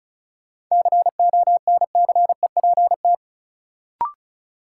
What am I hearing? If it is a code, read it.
Morse code, heard as a keyed tone, about 34 wpm, 700 Hz. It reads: CONCEPT
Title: Sentences from Top 1100 Words - New Word Drill Alt - 35wpm